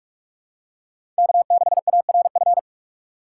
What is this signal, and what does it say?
Morse code, heard as a keyed tone, about 46 wpm, 700 Hz. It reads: K6ARF
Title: USA Call Signs - 6x Character Spacing - 45wpm